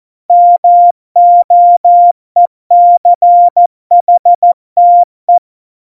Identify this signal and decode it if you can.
Morse code, heard as a keyed tone, about 14 wpm, 700 Hz. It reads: MOECHTE
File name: Morsekurs / Witze / Witz 03 April 2018, 70 F80